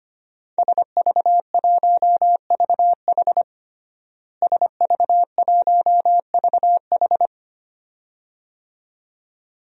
Morse code, heard as a keyed tone, about 25 wpm, 700 Hz. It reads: SV1VH SV1VH
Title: DXCC-100 Call Signs - 5 Characters - Repeated 2x - 25wpm